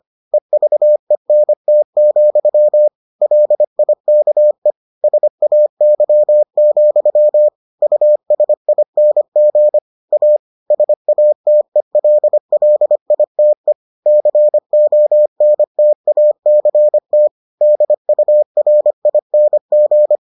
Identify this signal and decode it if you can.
Morse code, heard as a keyed tone, about 25 wpm, 600 Hz. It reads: EVENT, LIKE SAY, USING A SATELLITE CONTACT DURING